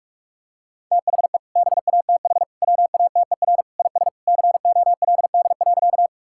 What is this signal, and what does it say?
Morse code, heard as a keyed tone, about 46 wpm, 700 Hz. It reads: THE BATH WATER IS COLD.
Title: Sentences from Top 1600 Words 45wpm